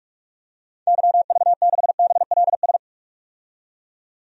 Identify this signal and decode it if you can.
Morse code, heard as a keyed tone, about 44 wpm, 690 Hz. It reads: YV6BLS
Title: DXCC-100 Call Signs - 6 Characters - 45wpm